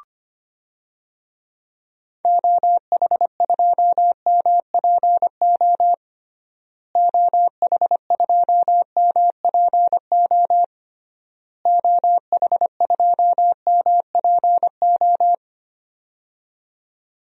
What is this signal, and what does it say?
Morse code, heard as a keyed tone, about 25 wpm, 700 Hz. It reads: OH2MPO OH2MPO OH2MPO
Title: DXCC-100 Call Signs - 6 Characters - Repeated 3x - 25wpm